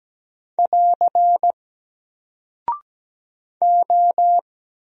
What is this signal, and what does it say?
Morse code, heard as a keyed tone, about 17 wpm, 700 Hz. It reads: <AR> O
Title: Single Character (Letter:Number:Punctuation:Prosign) Alt 17wpm